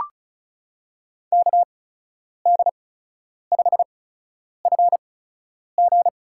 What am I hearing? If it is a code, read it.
Morse code, heard as a keyed tone, about 36 wpm, spaced as for 10 wpm, 700 Hz. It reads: KD5FG
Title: USA Call Signs - 8x Character Spacing - 35wpm